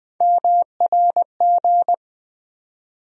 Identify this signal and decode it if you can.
Morse code, heard as a keyed tone, about 20 wpm, 700 Hz. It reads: MRG
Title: USA Call Sign Postfixes - 3 Letters 20wpm